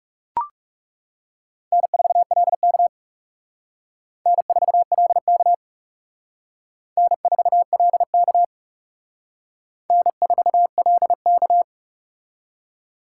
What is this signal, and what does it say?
Morse code, heard as a keyed tone, about 37 wpm, 700 Hz. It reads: N4LK N4LK N4LK N4LK
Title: Speed Racing - USA 1x2 Call Signs 30wpm